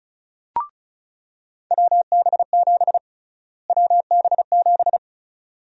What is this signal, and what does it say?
Morse code, heard as a keyed tone, about 36 wpm, 700 Hz. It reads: WB7 WB7
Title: USA Call Sign Prefixes - Repeated 2x - 35wpm